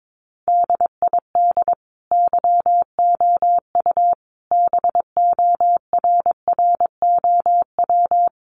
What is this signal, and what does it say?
Morse code, heard as a keyed tone, about 22 wpm, 700 Hz. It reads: DID YOU BORROW
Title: Sentences from Top 1500 Words 22wpm